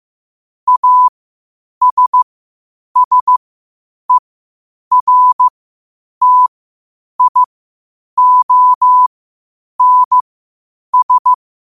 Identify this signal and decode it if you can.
Morse code, heard as a keyed tone, about 15 wpm, spaced as for 8 wpm, 1000 Hz. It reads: ASSERTIONS
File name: qotdfiles / 8wpm